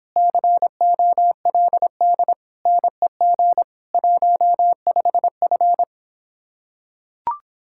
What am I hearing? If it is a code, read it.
Morse code, heard as a keyed tone, about 26 wpm, 700 Hz. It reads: COLD NEG 15F